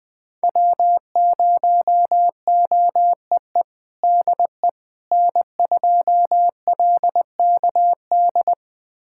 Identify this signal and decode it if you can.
Morse code, heard as a keyed tone, about 20 wpm, 700 Hz. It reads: W0OEE DE N2LKD
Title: USA Callsign DE Callsign - 5 Characters - 20wpm